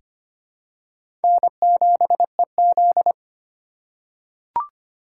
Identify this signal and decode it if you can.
Morse code, heard as a keyed tone, about 25 wpm, 700 Hz. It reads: N7EZ
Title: USA Call Signs - 4 Characters - 4x Character Spacing - 25wpm